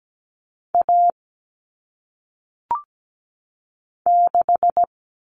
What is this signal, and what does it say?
Morse code, heard as a keyed tone, about 17 wpm, 700 Hz. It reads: A 6
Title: Common Cut Numbers - 1-Digit 17wpm